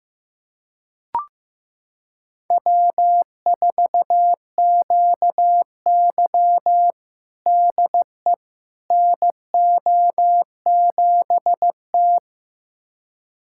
Morse code, heard as a keyed tone, about 16 wpm, 700 Hz. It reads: W4QY DE NO7T
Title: USA Callsign DE Callsign - 4 Characters - 15wpm